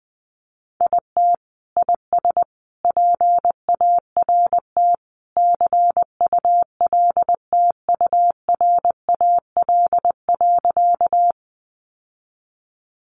Morse code, heard as a keyed tone, about 20 wpm, 700 Hz. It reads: IT IS PART CULTURAL.
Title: Sentences from Top 1500 Words 20wpm